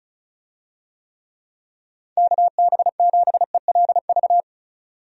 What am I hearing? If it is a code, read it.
Morse code, heard as a keyed tone, about 34 wpm, 700 Hz. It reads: KB7ELV
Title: USA Call Signs - 6 Characters - 35wpm